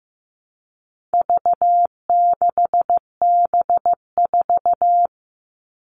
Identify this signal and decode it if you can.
Morse code, heard as a keyed tone, about 15 wpm, 700 Hz. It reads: V6B4
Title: All Cut Numbers - 4-Digit 15wpm